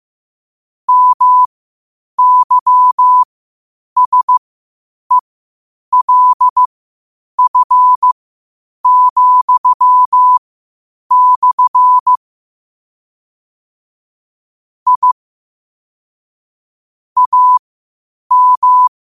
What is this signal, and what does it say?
Morse code, heard as a keyed tone, about 15 wpm, spaced as for 8 wpm, 1000 Hz. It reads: MYSELF,/ I AM
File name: qotdfiles / 8wpm